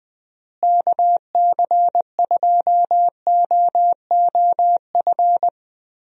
Morse code, heard as a keyed tone, about 21 wpm, 700 Hz. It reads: KC2OOF
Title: USA Call Signs - 6 Characters - Repeated 3x - 20wpm